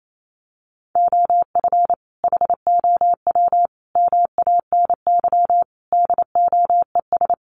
Morse code, heard as a keyed tone, about 28 wpm, 700 Hz. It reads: OF HOW MANY DOES
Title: Sentences from Top 1400 Words 28wpm